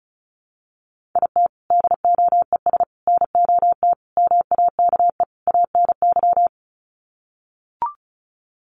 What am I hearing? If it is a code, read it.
Morse code, heard as a keyed tone, about 34 wpm, 700 Hz. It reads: IT DOES NOT MAKE ANY
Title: Sets of 5 Words Encore2 35wpm